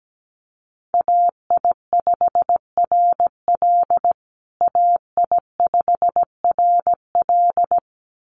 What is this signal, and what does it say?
Morse code, heard as a keyed tone, about 17 wpm, 700 Hz. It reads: AI5RL AI5RL
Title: USA Call Signs Repeated 2x 17wpm